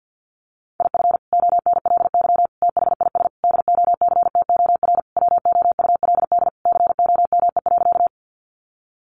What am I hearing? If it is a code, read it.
Morse code, heard as a keyed tone, about 50 wpm, 700 Hz. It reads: IF ONLY THIS DOCTOR WOULD COME.
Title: Sentences from Top 1500 Words 50wpm